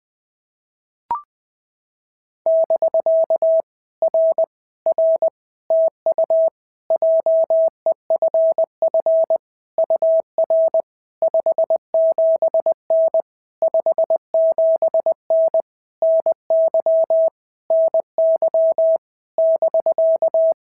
Morse code, heard as a keyed tone, about 21 wpm, 650 Hz. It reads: <BK> R R TU JEFF UR 57N 57N NY NY <BK>
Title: POTA QSO Practice - 20wpm